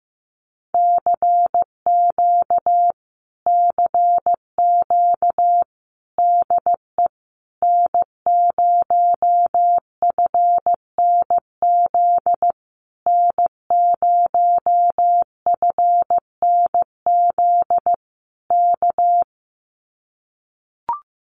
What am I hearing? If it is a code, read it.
Morse code, heard as a keyed tone, about 15 wpm, 700 Hz. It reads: CQ CQ DE N0FNZ N0FNZ K
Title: Common QSO Phrases Encore 15wpm